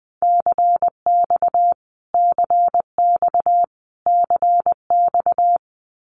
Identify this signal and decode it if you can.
Morse code, heard as a keyed tone, about 20 wpm, 700 Hz. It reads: CX CX CX
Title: Abbreviations Repeated 3x - 20wpm